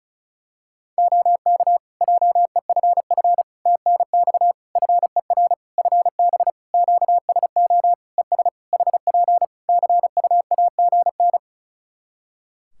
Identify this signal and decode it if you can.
Morse code, heard as a keyed tone, about 36 wpm, 700 Hz. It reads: OK JEFF TNX FER FB QSO ES HP CUAGN
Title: Practice QSO - Exchange 4 - Ending - 35wpm